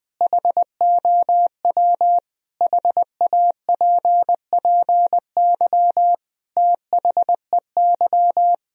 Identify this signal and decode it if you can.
Morse code, heard as a keyed tone, about 21 wpm, 700 Hz. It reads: HOW HAPPY THEY
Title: Sentences from Top 1000 Words - Review Part 3 - 20wpm